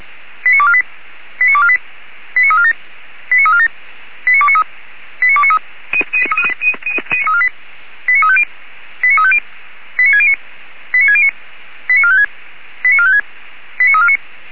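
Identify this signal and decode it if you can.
Five-tone selective call (ZVEI), heard as quick runs of five tones, tones of about 70 ms. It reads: 89248 89248 89347 89347 89293 89293 89350 89350 89609 89609 89456 89456 89349
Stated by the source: Five-tone ZVEI selective call received by SDR, from the server of Feuerwehr Orlamünde.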